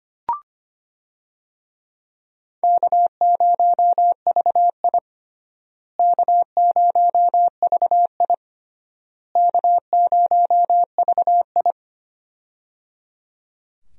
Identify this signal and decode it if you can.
Morse code, heard as a keyed tone, about 25 wpm, 700 Hz. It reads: K0VI K0VI K0VI
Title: USA Call Signs - 4 Characters - Repeated 3x - 25wpm